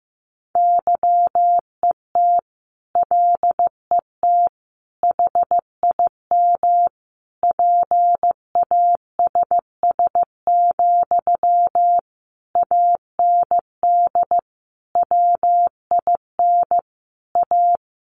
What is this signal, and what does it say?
Morse code, heard as a keyed tone, about 15 wpm, 700 Hz. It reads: YET LET HIM PASS, AND WIN A
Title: Sentences from Top 1500 Words 15wpm